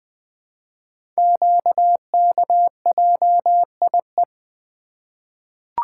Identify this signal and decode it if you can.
Morse code, heard as a keyed tone, about 21 wpm, 700 Hz. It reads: QKJIE
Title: Code Groups - 5 Letters - 20wpm